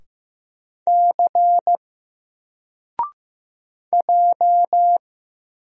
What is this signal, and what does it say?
Morse code, heard as a keyed tone, about 15 wpm, 700 Hz. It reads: C J
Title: Single Character (Letter:Number:Punctuation:Prosign) Alt 15wpm